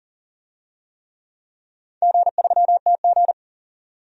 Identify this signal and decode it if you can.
Morse code, heard as a keyed tone, about 42 wpm, 700 Hz. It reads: G3TG
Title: DXCC-100 Call Signs - 4 Characters - 40wpm